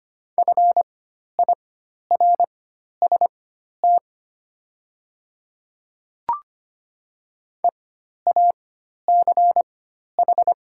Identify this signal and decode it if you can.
Morse code, heard as a keyed tone, about 25 wpm, spaced as for 12 wpm, 700 Hz. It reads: FIRST EACH
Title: 4x Character Spacing - Top 100 Words - 25wpm